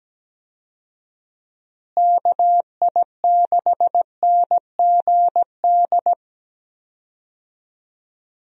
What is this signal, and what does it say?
Morse code, heard as a keyed tone, about 17 wpm, 700 Hz. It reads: KI6NGD
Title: USA Call Signs - 6 Characters - Repeated 3x - 17wpm